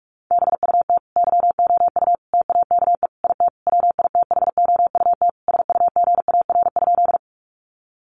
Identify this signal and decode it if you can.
Morse code, heard as a keyed tone, about 44 wpm, 700 Hz. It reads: BUT YOU TAKE IT WITHOUT SUGAR?
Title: Sentences from Top 1200 Words 45wpm